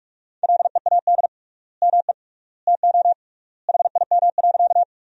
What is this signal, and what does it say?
Morse code, heard as a keyed tone, about 47 wpm, 700 Hz. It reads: LEAD ME TO HIM.
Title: 3x Word-Spacing - Sentences from Top 500 Words - 45wpm